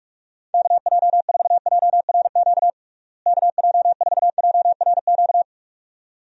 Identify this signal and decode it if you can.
Morse code, heard as a keyed tone, about 46 wpm, 700 Hz. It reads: KJ4JRQ KJ4JRQ
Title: USA Call Signs - 6 Characters - Repeated 2x - 45wpm